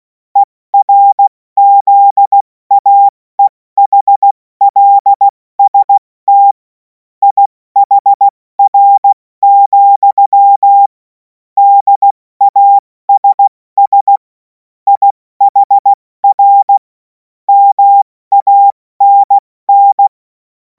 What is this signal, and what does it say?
Morse code, heard as a keyed tone, about 16 wpm, 800 Hz. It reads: ERZAEHLST IHR, DASS IHR MANN
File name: Morsekurs / Witze / Witz 07. April 2026 70 F80